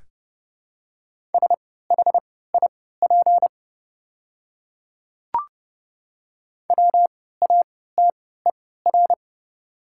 Morse code, heard as a keyed tone, about 30 wpm, spaced as for 17 wpm, 700 Hz. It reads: SHIP WATER
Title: 3x Character Spacing - Top 500 Words - 30wpm